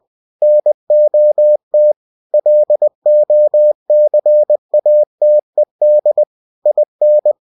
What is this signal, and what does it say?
Morse code, heard as a keyed tone, about 20 wpm, 600 Hz. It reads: NOT LOCATED IN
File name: winnie-20wpm